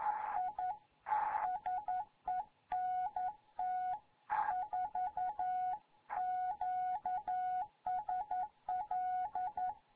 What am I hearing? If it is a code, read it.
Morse code, heard as a keyed tone, about 11 wpm, 740 Hz. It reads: I SENT 4 QSL
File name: Morze-HA2MN